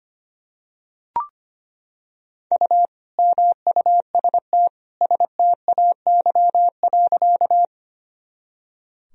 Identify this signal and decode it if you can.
Morse code, heard as a keyed tone, about 25 wpm, 700 Hz. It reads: U MUST STAY.
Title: Sentences from Top 500 Words and Abbreviations 25wpm